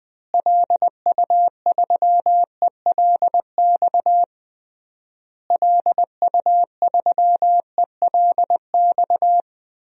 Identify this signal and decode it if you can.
Morse code, heard as a keyed tone, about 21 wpm, 700 Hz. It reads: LU3ELX LU3ELX
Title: DXCC-100 Call Signs - 6 Characters - Repeated 3x - 20wpm